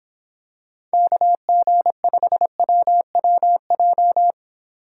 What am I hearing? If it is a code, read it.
Morse code, heard as a keyed tone, about 26 wpm, 700 Hz. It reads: KG5WWJ